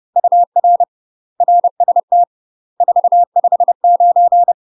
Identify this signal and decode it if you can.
Morse code, heard as a keyed tone, about 30 wpm, 700 Hz. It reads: UR RST 459
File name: UK MASTER QSOs TRAINING / 30WPM / UK MASTER QSOs 0011